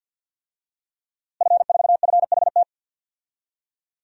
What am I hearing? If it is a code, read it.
Morse code, heard as a keyed tone, about 47 wpm, 700 Hz. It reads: F4FHT
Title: DXCC-100 Call Signs - 5 Characters - Repeated 2x - 50wpm